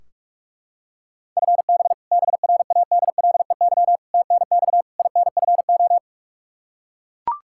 Morse code, heard as a keyed tone, about 41 wpm, 700 Hz. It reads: FB BRADLEY TNX INFO
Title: Practice QSO - Exchange 3 - More Info - 45wpm